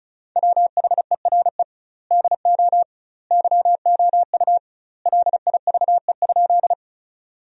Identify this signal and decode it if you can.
Morse code, heard as a keyed tone, about 34 wpm, 700 Hz. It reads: WHERE DO YOU LIVE?
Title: 2x Word-Spacing - Sentences from Top 200 Words - 35wpm